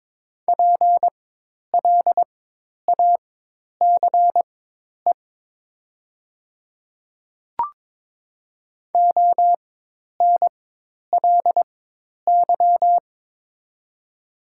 Morse code, heard as a keyed tone, about 22 wpm, spaced as for 10 wpm, 700 Hz. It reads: PLACE ONLY
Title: 4x Character Spacing - Top 200 Words - 22wpm